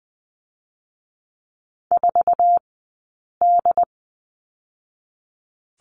Morse code, heard as a keyed tone, about 20 wpm, 700 Hz. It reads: 4 D
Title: 066 Lesson 12d - Letter d - Two Characters 20wpm